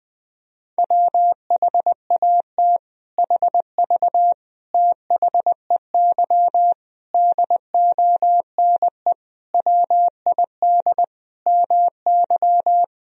Morse code, heard as a keyed tone, about 20 wpm, 700 Hz. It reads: WHAT HV THEY DONE WID MY
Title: Sentences from Top 500 Words and Abbreviations 20wpm